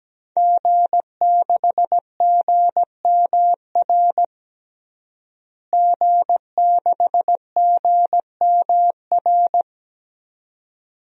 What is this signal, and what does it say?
Morse code, heard as a keyed tone, about 17 wpm, 700 Hz. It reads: G6GMR G6GMR
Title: DXCC-100 Call Signs - 5 Characters - Repeated 2x - 17wpm